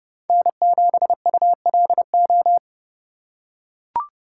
Morse code, heard as a keyed tone, about 30 wpm, 700 Hz. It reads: N7ULO
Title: USA Call Signs - 5 Characters - 2x Character Spacing - 30wpm